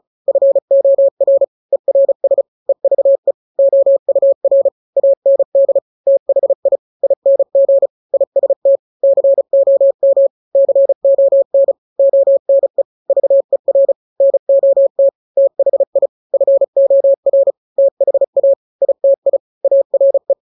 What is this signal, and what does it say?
Morse code, heard as a keyed tone, about 34 wpm, 550 Hz. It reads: FOR ERS EVE OUR AND THI ING IST COM CON ONE VER NOT THI FOR THA ITI ARE
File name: 3-grams 35 30 WPM